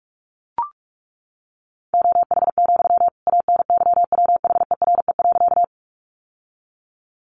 Morse code, heard as a keyed tone, about 44 wpm, 700 Hz. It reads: OH, ANYWHERE.
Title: Sentences from Top 1400 Words 45wpm